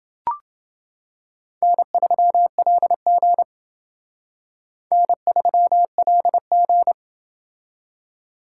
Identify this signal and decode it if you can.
Morse code, heard as a keyed tone, about 28 wpm, 700 Hz. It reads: N3LG N3LG